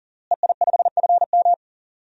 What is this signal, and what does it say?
Morse code, heard as a keyed tone, about 37 wpm, 700 Hz. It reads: EI5FM
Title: DXCC-100 Call Signs - 5 Characters - Repeated 2x - 40wpm